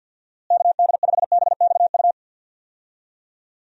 Morse code, heard as a keyed tone, about 47 wpm, 700 Hz. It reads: KD5BXU
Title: USA Call Signs - 6 Characters - Repeated 3x - 50wpm